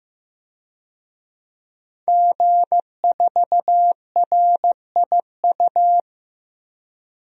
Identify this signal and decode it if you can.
Morse code, heard as a keyed tone, about 16 wpm, 700 Hz. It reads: G4RIU
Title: DXCC-100 Call Signs - 5 Characters - 15wpm